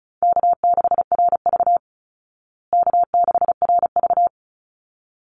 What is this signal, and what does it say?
Morse code, heard as a keyed tone, about 34 wpm, 700 Hz. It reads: K6RV K6RV
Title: Easy Call Signs Repeated 2x - 35wpm